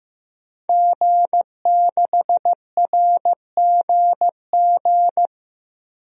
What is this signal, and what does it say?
Morse code, heard as a keyed tone, about 15 wpm, 700 Hz. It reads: G6RGG
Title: DXCC-100 Call Signs - 5 Characters - 15wpm